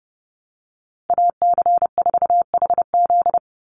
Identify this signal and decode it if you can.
Morse code, heard as a keyed tone, about 30 wpm, 700 Hz. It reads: AC4HZ
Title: USA Call Signs 30wpm